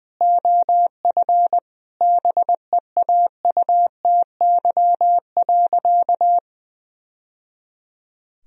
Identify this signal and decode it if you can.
Morse code, heard as a keyed tone, about 20 wpm, 700 Hz. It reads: OF BEAUTY.
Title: Sentences from Top 1000 Words - Review Part 2 - 20wpm